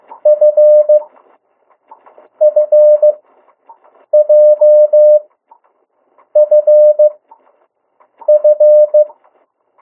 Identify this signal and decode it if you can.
Morse code, heard as a keyed tone, about 15 wpm, spaced as for 6 wpm, 600 Hz. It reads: FFJFF
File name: morse mp3 / 02 urok 1